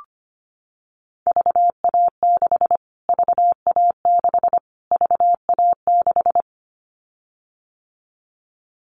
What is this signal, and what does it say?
Morse code, heard as a keyed tone, about 25 wpm, 700 Hz. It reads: VA6 VA6 VA6
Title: Callsign Prefixes for Top 25 DXCC - Repeated 3x - 25wpm